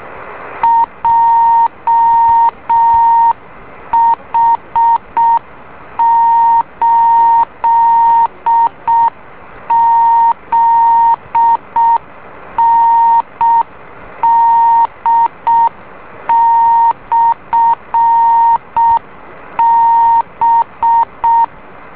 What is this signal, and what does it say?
Morse code, heard as a keyed tone, about 6 wpm, 920 Hz. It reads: JH8ZND/B
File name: beacon / JH8ZND